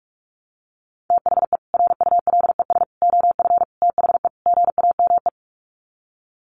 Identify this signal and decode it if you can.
Morse code, heard as a keyed tone, about 44 wpm, 700 Hz. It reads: THE RULES OF THE GAME
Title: Sets of 5 Words Encore2 45wpm